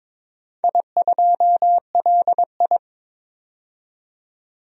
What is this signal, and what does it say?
Morse code, heard as a keyed tone, about 22 wpm, 700 Hz. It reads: I2LI
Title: DXCC-100 Call Signs - 4 Characters - 22wpm